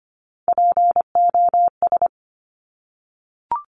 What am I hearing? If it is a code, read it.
Morse code, heard as a keyed tone, about 25 wpm, 700 Hz. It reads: POS
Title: Top Three Letter Combinations in Words 25wpm